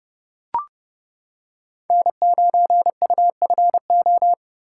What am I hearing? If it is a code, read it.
Morse code, heard as a keyed tone, about 28 wpm, 700 Hz. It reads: N9UFO